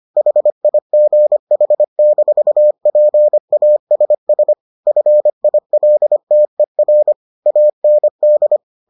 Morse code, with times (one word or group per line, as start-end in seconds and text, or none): 0.2-4.5 HIGH-PASS
4.9-7.1 FILTER
7.5-8.6 AND